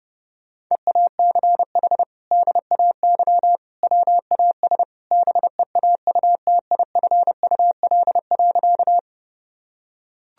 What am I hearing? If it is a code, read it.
Morse code, heard as a keyed tone, about 30 wpm, 700 Hz. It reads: EACH DAY WAS BEAUTIFUL.
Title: Sentences from Top 1000 Words - Review Part 1 - 30wpm